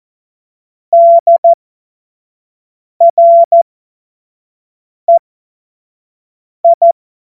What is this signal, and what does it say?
Morse code, heard as a keyed tone, about 14 wpm, spaced as for 5 wpm, 680 Hz. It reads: DREI
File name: Morsekurs / Witze / Witz 27. September 2022 25 F70